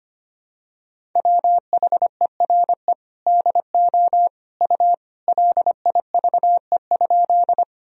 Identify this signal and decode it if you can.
Morse code, heard as a keyed tone, about 25 wpm, 700 Hz. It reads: WHERE DO U LIVE?
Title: Sentences from Top 200 Words and Abbreviations 25wpm